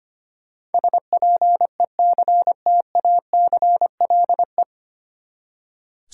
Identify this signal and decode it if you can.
Morse code, heard as a keyed tone, about 25 wpm, 700 Hz. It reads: SPECTACLE
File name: Nine Letter Words - 25wpm